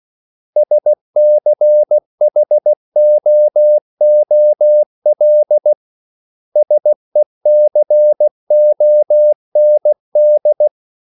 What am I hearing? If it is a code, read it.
Morse code, heard as a keyed tone, about 16 wpm, 600 Hz. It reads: SCHOOL SECOND
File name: CW / Practice / 500MostCommonEnglishWords16